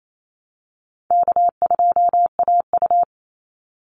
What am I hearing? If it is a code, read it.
Morse code, heard as a keyed tone, about 28 wpm, 700 Hz. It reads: K2AU
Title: Easy Call Signs - 28wpm